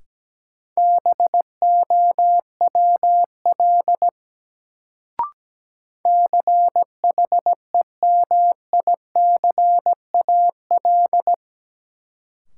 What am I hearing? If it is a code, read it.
Morse code, heard as a keyed tone, about 17 wpm, 700 Hz. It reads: BOWL CHEMICAL
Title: Sentences from Top 1200 Words - New Word Drill Alt - 17wpm